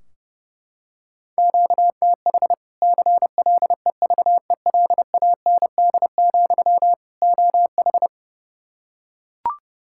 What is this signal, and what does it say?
Morse code, heard as a keyed tone, about 32 wpm, 700 Hz. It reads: QTH CLEVELAND, OH
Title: Practice QSO - Exchange 1 - Beginning - 30wpm